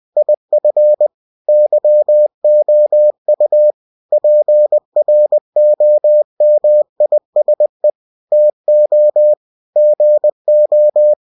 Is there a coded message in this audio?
Yes: Morse code from 0.2 to 11.1 s, heard as a keyed tone, about 20 wpm, 600 Hz. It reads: IF YOU PROMISE TO GO